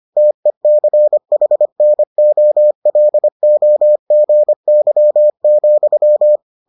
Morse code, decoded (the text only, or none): TECHNOLOGY,